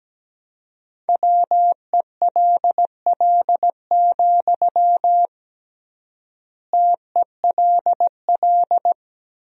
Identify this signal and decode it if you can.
Morse code, heard as a keyed tone, about 17 wpm, 700 Hz. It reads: WELL, TELL
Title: 3x Word-Spacing - Sentences from Top 200 Words - 17wpm